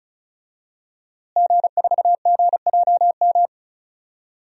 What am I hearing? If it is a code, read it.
Morse code, heard as a keyed tone, about 36 wpm, 700 Hz. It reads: G4GJM
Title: DXCC-100 Call Signs - 5 Characters - Repeated 2x - 35wpm